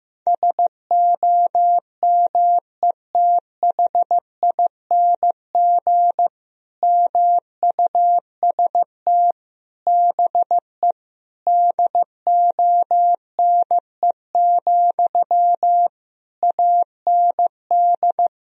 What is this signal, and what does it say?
Morse code, heard as a keyed tone, about 15 wpm, 700 Hz. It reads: SOMETHING MUST BE DONE, AND
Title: Sentences from Top 1000 Words - Review Part 1 - 15wpm